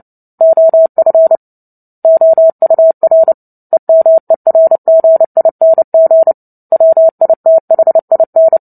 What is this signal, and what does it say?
Morse code, heard as a keyed tone, about 30 wpm, 650 Hz. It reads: OF OUR EMERGING WITHIN
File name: Book Two / 03-Chapter 3 - 29 WPM